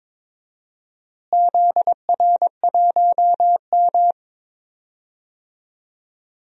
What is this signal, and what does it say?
Morse code, heard as a keyed tone, about 22 wpm, 700 Hz. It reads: ZR1M
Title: DXCC-100 Call Signs - 4 Characters - Repeated 3x - 22wpm